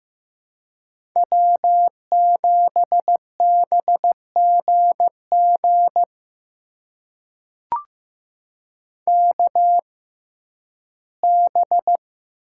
Morse code, heard as a keyed tone, about 15 wpm, spaced as for 11 wpm, 700 Hz. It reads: W7BGG KB
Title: USA Call Signs - 5 Characters - 6x Character Spacing - 15wpm